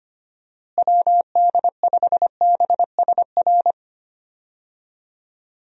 Morse code, heard as a keyed tone, about 25 wpm, 700 Hz. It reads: WD5BSR
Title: USA Call Signs - 6 Characters - Repeated 3x - 25wpm